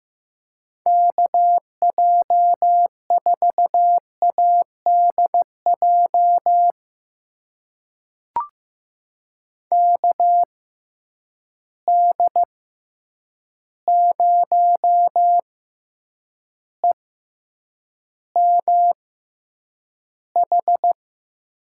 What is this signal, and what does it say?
Morse code, heard as a keyed tone, about 15 wpm, spaced as for 8 wpm, 700 Hz. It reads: KJ4ADJ KD0EMH
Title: USA Call Signs - 6 Characters - 6x Character Spacing - 15wpm